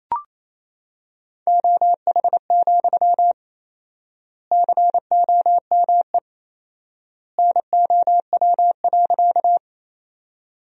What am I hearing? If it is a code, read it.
Morse code, heard as a keyed tone, about 28 wpm, 700 Hz. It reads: OH, COME NOW.
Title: 4x Word-Spacing - Sentences from Top 500 Words - 28wpm